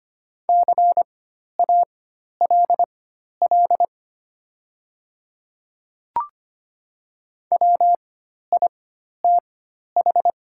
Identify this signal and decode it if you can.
Morse code, heard as a keyed tone, about 25 wpm, spaced as for 12 wpm, 700 Hz. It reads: CALL WITH
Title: 4x Character Spacing - Top 100 Words - 25wpm